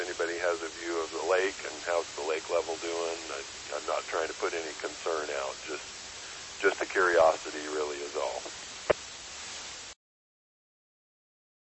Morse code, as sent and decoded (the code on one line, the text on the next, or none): none